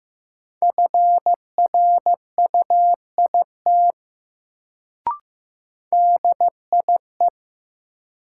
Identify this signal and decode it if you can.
Morse code, heard as a keyed tone, about 15 wpm, 700 Hz. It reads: FRUIT DIE
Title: Sentences from Top 1300 Words - New Word Drill - 15wpm